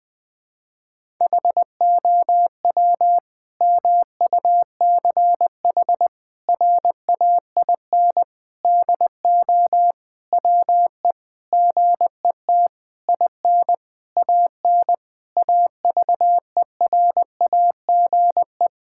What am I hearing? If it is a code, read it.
Morse code, heard as a keyed tone, about 20 wpm, 700 Hz. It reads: HOW MUCH RAIN DO WE GET IN AN AVERAGE
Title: Sentences from Top 1000 Words - Review Part 6 - 20wpm